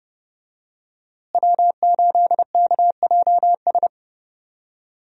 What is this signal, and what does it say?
Morse code, heard as a keyed tone, about 29 wpm, 700 Hz. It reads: W8KJS
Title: USA Call Signs - 5 Characters - Repeated 3x - 30wpm